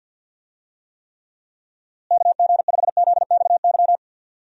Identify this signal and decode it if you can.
Morse code, heard as a keyed tone, about 51 wpm, 700 Hz. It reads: KG5ZXY
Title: USA Call Signs - 6 Characters - Repeated 2x - 50wpm